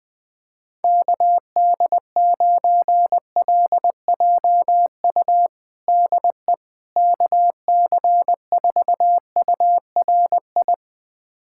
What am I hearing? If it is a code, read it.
Morse code, heard as a keyed tone, about 20 wpm, 700 Hz. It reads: KD9LJU DE KC4URI
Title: USA Callsign DE Callsign 20wpm